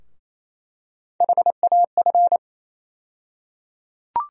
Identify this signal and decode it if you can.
Morse code, heard as a keyed tone, about 28 wpm, 700 Hz. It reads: HAF
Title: USA Call Sign Postfixes - 3 Letters 28wpm